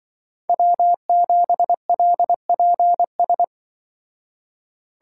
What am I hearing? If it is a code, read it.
Morse code, heard as a keyed tone, about 24 wpm, 700 Hz. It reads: W7LPS
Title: Speed Racing - USA 1x3 Call Signs 20wpm